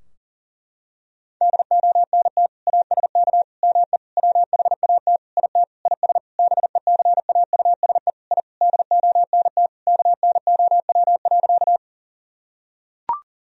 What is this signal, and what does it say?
Morse code, heard as a keyed tone, about 40 wpm, 700 Hz. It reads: DONT ASK ME WHAT IT IS BECAUSE I DONT KNOW.
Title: Sentences from Top 1000 Words - Review Part 3 - 40wpm